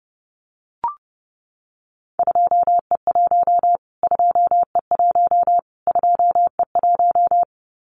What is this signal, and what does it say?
Morse code, heard as a keyed tone, about 30 wpm, 700 Hz. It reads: 2E1 2E1 2E1
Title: Callsign Prefixes for Top 25 DXCC - Repeated 3x - 30wpm